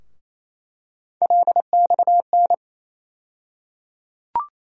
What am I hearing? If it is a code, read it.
Morse code, heard as a keyed tone, about 28 wpm, 700 Hz. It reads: LXN
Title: USA Call Sign Postfixes - 3 Letters 28wpm